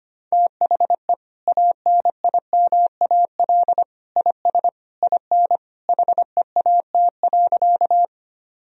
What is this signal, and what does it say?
Morse code, heard as a keyed tone, about 25 wpm, 700 Hz. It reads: THE ANIMAL IS IN HEAT.
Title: Sentences from Top 1000 Words - Review Part 3 - 25wpm